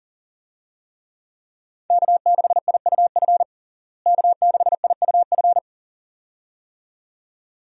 Morse code, heard as a keyed tone, about 40 wpm, 700 Hz. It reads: K6IUF K6IUF
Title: USA Call Signs - 5 Characters - Repeated 2x - 40wpm